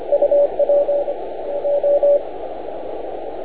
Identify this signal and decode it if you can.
Morse code, heard as a keyed tone, about 25 wpm, 570 Hz. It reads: UW O